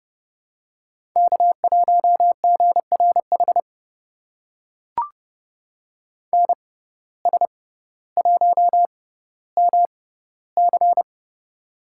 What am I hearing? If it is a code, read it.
Morse code, heard as a keyed tone, about 29 wpm, spaced as for 15 wpm, 700 Hz. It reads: K1GRH NS1MC
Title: USA Call Signs - 5 Characters - 6x Character Spacing - 30wpm